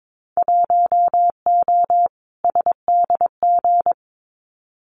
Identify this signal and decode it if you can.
Morse code, heard as a keyed tone, about 22 wpm, 700 Hz. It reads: 1O SDG
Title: Winter Field Day Exchanges Encore 22wpm